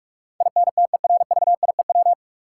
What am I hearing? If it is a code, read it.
Morse code, heard as a keyed tone, about 45 wpm, 700 Hz. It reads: INTERVIEW
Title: Sentences from Top 1600 Words - New Word Drill Alt - 45wpm